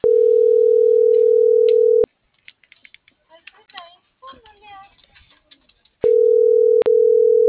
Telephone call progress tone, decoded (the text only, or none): ringback tone